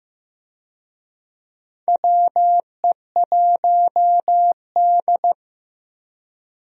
Morse code, heard as a keyed tone, about 14 wpm, 700 Hz. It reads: WE1D
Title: USA Call Signs - 4 Characters - Repeated 3x - 15wpm